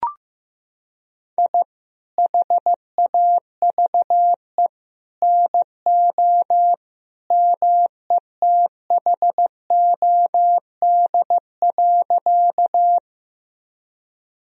Morse code, heard as a keyed tone, about 16 wpm, 700 Hz. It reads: I HAVE NO METHOD.
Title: Sentences from Top 1000 Words - Review Part 3 - 15wpm